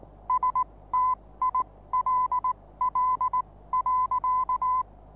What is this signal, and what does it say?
Morse code, heard as a keyed tone, about 19 wpm, spaced as for 16 wpm, 1010 Hz. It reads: STILL.
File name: repos / Morsekurs / DJ4UF Morsen26